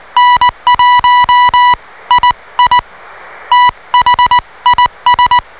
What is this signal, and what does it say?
Morse code, heard as a keyed tone, about 20 wpm, spaced as for 17 wpm, 980 Hz. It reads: N1II THIS